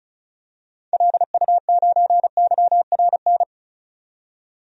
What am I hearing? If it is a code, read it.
Morse code, heard as a keyed tone, about 34 wpm, 700 Hz. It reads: LU9YRN
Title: DXCC-100 Call Signs - 6 Characters - 35wpm